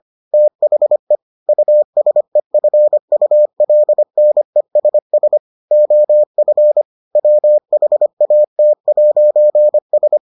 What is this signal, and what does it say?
Morse code, heard as a keyed tone, about 25 wpm, 600 Hz. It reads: THE USEFULNESS OF WHAT'S